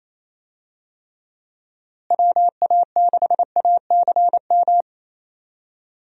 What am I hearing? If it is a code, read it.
Morse code, heard as a keyed tone, about 28 wpm, 700 Hz. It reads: WA6ACM
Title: USA Call Signs - 6 Characters - Repeated 3x - 28wpm